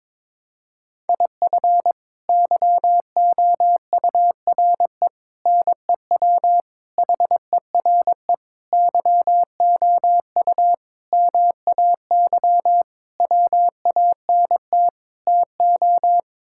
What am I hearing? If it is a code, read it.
Morse code, heard as a keyed tone, about 22 wpm, 700 Hz. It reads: IF YOURE NEW HERE YOU MAY WANT TO
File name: practice / 22wpm / Sets of 8 Words - 22wpm